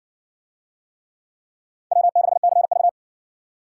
Morse code, heard as a keyed tone, about 56 wpm, 700 Hz. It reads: W6CV